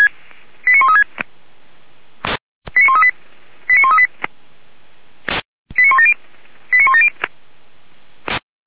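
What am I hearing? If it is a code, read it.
Five-tone selective call (ZVEI), heard as quick runs of five tones, tones of about 70 ms. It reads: 89136 89138 89138 89170 89170
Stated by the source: Five-tone ZVEI selective call received by SDR, from the fire brigade (Feuerwehr) Orlamünde server, channel 464.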